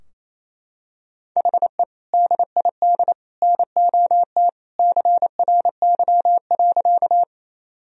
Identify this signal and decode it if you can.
Morse code, heard as a keyed tone, about 27 wpm, 700 Hz. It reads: HE DID NOT CRY.
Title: Sentences from Top 1000 Words - Review Part 2 - 28wpm